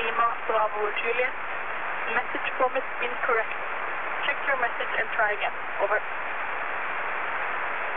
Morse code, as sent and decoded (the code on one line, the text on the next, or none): none